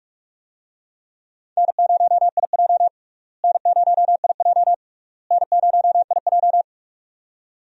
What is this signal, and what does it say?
Morse code, heard as a keyed tone, about 46 wpm, 700 Hz. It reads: N0IJ N0IJ N0IJ
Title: USA Call Signs - 4 Characters - Repeated 3x - 45wpm